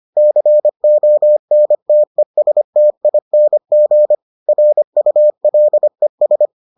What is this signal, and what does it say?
Morse code, heard as a keyed tone, about 25 wpm, 600 Hz. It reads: CONTESTING RULES